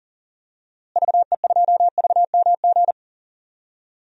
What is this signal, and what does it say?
Morse code, heard as a keyed tone, about 40 wpm, 700 Hz. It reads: VE2VMG
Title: DXCC-100 Call Signs - 6 Characters - Repeated 2x - 40wpm